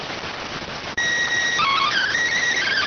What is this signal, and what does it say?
SSTV (slow-scan television), PD 290.